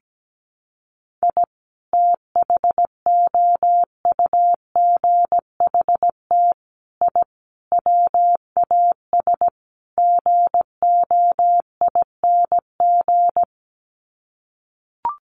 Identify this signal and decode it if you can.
Morse code, heard as a keyed tone, about 17 wpm, 700 Hz. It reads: I THOUGHT I WAS GOING
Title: Sets of 5 Words Encore2 17wpm